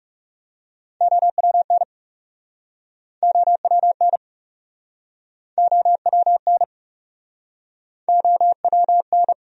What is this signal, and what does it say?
Morse code, heard as a keyed tone, about 40 wpm, 700 Hz. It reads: OWN OWN OWN OWN